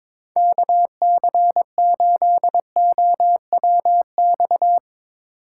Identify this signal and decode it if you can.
Morse code, heard as a keyed tone, about 22 wpm, 700 Hz. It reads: KC8OWX
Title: Speed Racing - USA 2x3 Call Signs 22wpm